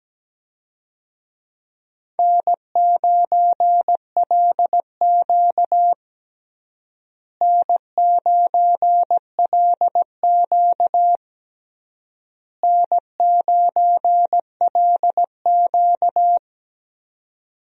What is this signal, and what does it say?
Morse code, heard as a keyed tone, about 17 wpm, 700 Hz. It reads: N9LQ N9LQ N9LQ
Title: USA Call Signs - 4 Characters - Repeated 3x - 17wpm